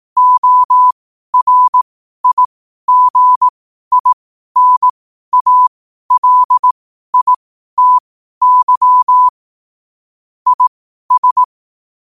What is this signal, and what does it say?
Morse code, heard as a keyed tone, about 18 wpm, spaced as for 12 wpm, 1000 Hz. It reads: ORIGINALITY IS
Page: 12-wpm-morse-code-quotes